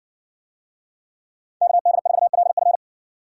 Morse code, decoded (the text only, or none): KD4LF